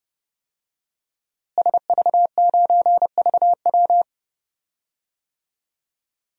Morse code, heard as a keyed tone, about 32 wpm, 700 Hz. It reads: SV9VW
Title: DXCC-100 Call Signs - 5 Characters - Repeated 3x - 30wpm